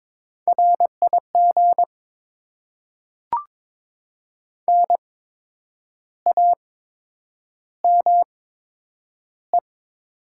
Morse code, heard as a keyed tone, about 22 wpm, spaced as for 8 wpm, 700 Hz. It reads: RIG NAME
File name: Top 20 QSO Elements - 8x Character Spacing - 22wpm